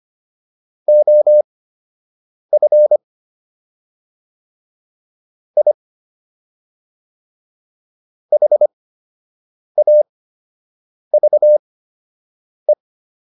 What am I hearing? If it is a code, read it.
Morse code, heard as a keyed tone, about 25 wpm, spaced as for 7 wpm, 600 Hz. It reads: OF I HAVE